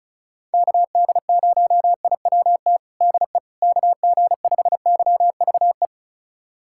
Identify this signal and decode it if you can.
Morse code, heard as a keyed tone, about 36 wpm, 700 Hz. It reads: KD0IWT DE KG5YVE
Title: USA Callsign DE Callsign - 6 Characters - 35wpm